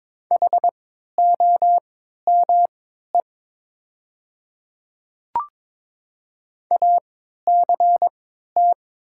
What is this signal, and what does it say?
Morse code, heard as a keyed tone, about 22 wpm, spaced as for 13 wpm, 700 Hz. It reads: HOME ACT
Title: 3x Character Spacing - Top 200 Words - 22wpm